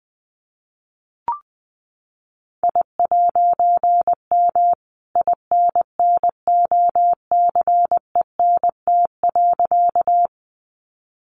Morse code, heard as a keyed tone, about 19 wpm, 700 Hz. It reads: I'M INNOCENT.
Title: A Sound Of Thunder 20wpm